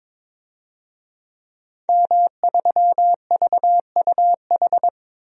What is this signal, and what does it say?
Morse code, heard as a keyed tone, about 22 wpm, 700 Hz. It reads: M3VUH
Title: DXCC-100 Call Signs - 5 Characters - 22wpm